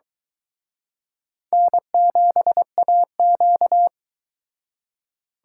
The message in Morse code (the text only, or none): N7AQ